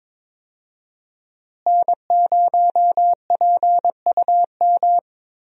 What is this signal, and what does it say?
Morse code, heard as a keyed tone, about 22 wpm, 700 Hz. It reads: N0PUM
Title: USA Call Signs - 5 Characters - Repeated 2x - 22wpm